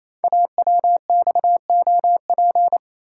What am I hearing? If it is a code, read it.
Morse code, heard as a keyed tone, about 27 wpm, 700 Hz. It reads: AWXOP
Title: Code Groups - 5 Letters - 28wpm